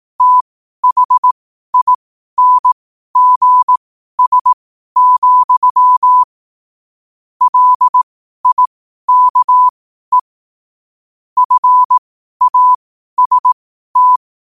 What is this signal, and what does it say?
Morse code, heard as a keyed tone, about 18 wpm, spaced as for 12 wpm, 1000 Hz. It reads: THINGS, LIKE FAST
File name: qotdfiles / 12wpm